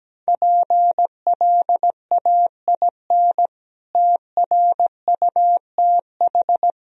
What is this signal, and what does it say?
Morse code, heard as a keyed tone, about 17 wpm, 700 Hz. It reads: PLAIN TRUTH
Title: Sentences from Top 1000 Words - Review Part 2 - 17wpm